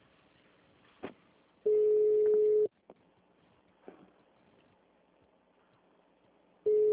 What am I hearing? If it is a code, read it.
Ringback tone (Europe, 425 Hz).
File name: out- -2004- - - .30